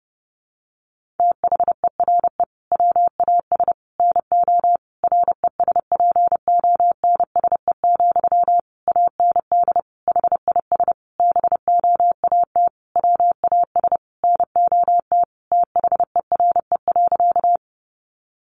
Morse code, heard as a keyed tone, about 30 wpm, 700 Hz. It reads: THERE WAS NO RESPONSE, AND HIS BOAT WAS NOT THERE.
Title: Sentences from Top 1400 Words 30wpm